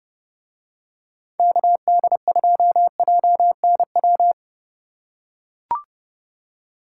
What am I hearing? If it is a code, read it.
Morse code, heard as a keyed tone, about 28 wpm, 700 Hz. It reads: KD2JNW